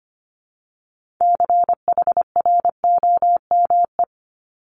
Morse code, heard as a keyed tone, about 25 wpm, 700 Hz. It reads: CHROME
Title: Multi-letter Phonemes Repeated 3x 25wpm